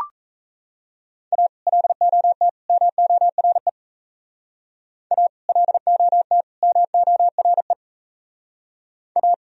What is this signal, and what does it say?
Morse code, heard as a keyed tone, about 34 wpm, 700 Hz. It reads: A LOT MORE A LOT MORE A
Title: Speed-Racing - Sets of 3 Words 28wpm